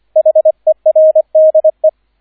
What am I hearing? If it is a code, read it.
Morse code, heard as a keyed tone, about 25 wpm, 630 Hz, with radio noise. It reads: HERDE